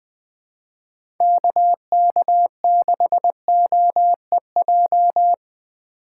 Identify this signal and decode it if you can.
Morse code, heard as a keyed tone, about 20 wpm, 700 Hz. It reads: KK6OEJ
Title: USA Call Signs - 4x Character Spacing - 20wpm